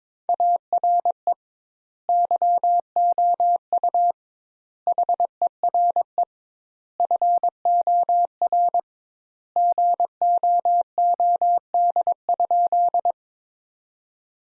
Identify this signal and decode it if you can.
Morse code, heard as a keyed tone, about 22 wpm, 700 Hz. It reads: ARE YOU HERE FOR GOOD?
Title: Sentences Top 200 Words - 22wpm